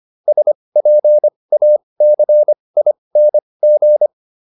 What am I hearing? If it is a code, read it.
Morse code, heard as a keyed tone, about 25 wpm, spaced as for 20 wpm, 600 Hz. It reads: SPACING